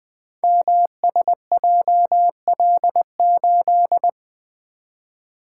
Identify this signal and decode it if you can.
Morse code, heard as a keyed tone, about 20 wpm, 700 Hz. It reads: MSJL8
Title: Code Groups - 5 Letter:Numbers - 20wpm